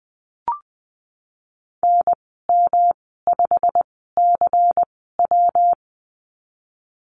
Morse code, heard as a keyed tone, about 20 wpm, spaced as for 15 wpm, 700 Hz. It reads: NM5CW
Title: 140 Lesson 23g - Letter f - Call Signs 20wpm